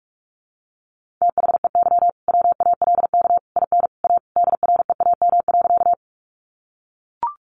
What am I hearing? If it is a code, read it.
Morse code, heard as a keyed tone, about 44 wpm, 700 Hz. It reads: THEY WALK IN A DREAM.
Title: Sentences from Top 1500 Words 45wpm